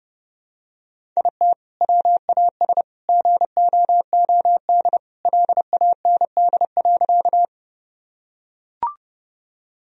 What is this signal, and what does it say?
Morse code, heard as a keyed tone, about 28 wpm, 700 Hz. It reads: IT WAS GOOD LAND.
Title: Sentences from Top 1000 Words - Review Part 1 - 30wpm